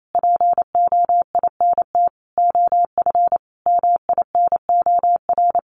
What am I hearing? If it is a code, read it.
Morse code, heard as a keyed tone, about 28 wpm, 700 Hz. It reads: POINT OF MINOR
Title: Sentences from Top 1500 Words 28wpm